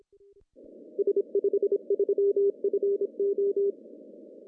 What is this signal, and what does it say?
Morse code, heard as a keyed tone, about 27 wpm, 400 Hz. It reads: S53FO
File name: CW / cqww-cw-k5tr